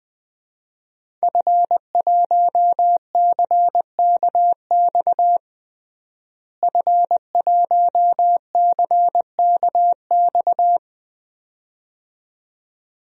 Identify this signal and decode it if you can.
Morse code, heard as a keyed tone, about 20 wpm, 700 Hz. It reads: F1CKX F1CKX
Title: DXCC-100 Call Signs - 5 Characters - Repeated 2x - 20wpm